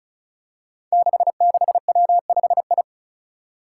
Morse code, heard as a keyed tone, about 36 wpm, 700 Hz. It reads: 66W5I
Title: Five Challenging Letter-Number Combinations 35wpm